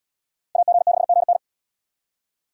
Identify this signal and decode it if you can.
Morse code, heard as a keyed tone, about 77 wpm, 700 Hz. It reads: AG6GN